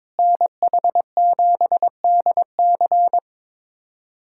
Morse code, heard as a keyed tone, about 22 wpm, 700 Hz. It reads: NH7DC
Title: USA Callsign DE Callsign - 5 Characters - 22wpm